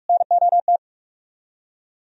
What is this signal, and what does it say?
Morse code, heard as a keyed tone, about 55 wpm, 700 Hz. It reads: NOT
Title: Speed-Racing - Sets of 2 Words 45wpm